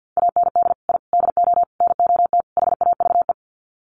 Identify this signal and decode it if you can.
Morse code, heard as a keyed tone, about 50 wpm, 700 Hz. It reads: AND I DO NOT HAVE
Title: Sets of 5 Words Encore 50wpm